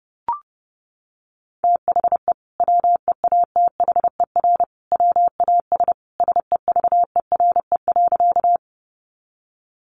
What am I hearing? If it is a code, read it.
Morse code, heard as a keyed tone, about 30 wpm, 700 Hz. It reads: THE WEATHER WAS SEVERE.
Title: Sentences from Top 1400 Words 30wpm